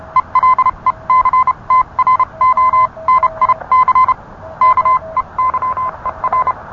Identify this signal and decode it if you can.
Morse code, heard as a keyed tone, about 30 wpm, spaced as for 26 wpm, 1030 Hz. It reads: ELECTRONIC KEYER